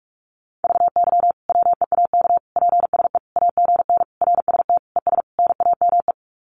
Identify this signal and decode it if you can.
Morse code, heard as a keyed tone, about 44 wpm, 700 Hz. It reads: VY WEAK PSE AGN RST ES NAME
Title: Common QSO Phrases 45wpm